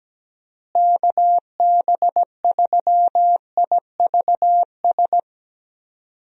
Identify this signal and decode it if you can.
Morse code, heard as a keyed tone, about 17 wpm, 700 Hz. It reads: KB3IVS
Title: USA Call Signs - 6 Characters - Repeated 2x - 17wpm